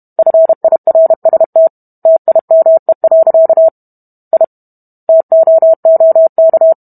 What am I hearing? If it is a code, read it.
Morse code, heard as a keyed tone, about 32 wpm, 650 Hz. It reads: FIRST TIME. I TOOK